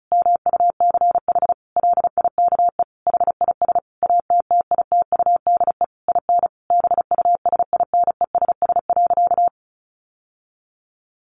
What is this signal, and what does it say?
Morse code, heard as a keyed tone, about 34 wpm, 700 Hz. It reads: MUCH LIKE HIS ATTITUDE IN BUSINESS.
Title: Sentences from Top 1300 Words 35wpm